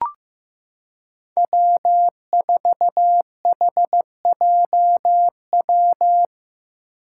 Morse code, heard as a keyed tone, about 15 wpm, 700 Hz. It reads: W4HJW
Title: Five Challenging Letter-Number Combinations 15wpm